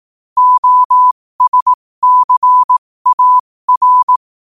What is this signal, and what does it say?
Morse code, heard as a keyed tone, about 18 wpm, spaced as for 15 wpm, 1000 Hz. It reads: OSCAR
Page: New quotes every day in morse code at 15 Words per minute.